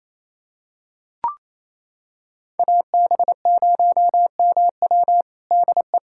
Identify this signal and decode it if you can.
Morse code, heard as a keyed tone, about 28 wpm, 700 Hz. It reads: AB0MW DE
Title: USA Callsign DE Callsign - 5 Characters - 28wpm